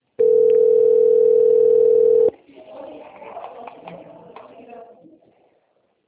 Ringback tone (North America), 440 + 480 Hz.
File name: out- - - - - .27187